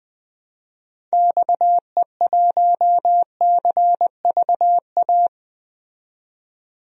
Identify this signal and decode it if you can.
Morse code, heard as a keyed tone, about 20 wpm, 700 Hz. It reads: XE1CVA
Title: DXCC-100 Call Signs - 6 Characters - 20wpm